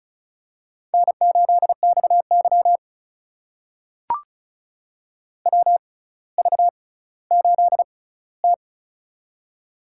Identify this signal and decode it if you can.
Morse code, heard as a keyed tone, about 36 wpm, spaced as for 18 wpm, 700 Hz. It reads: N8XY WV8T
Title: USA Call Signs - 4 Characters - 6x Character Spacing - 35wpm